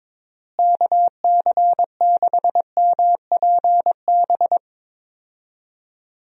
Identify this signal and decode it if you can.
Morse code, heard as a keyed tone, about 22 wpm, 700 Hz. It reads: KC6MPB
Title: USA Call Signs - 6 Characters - Repeated 3x - 22wpm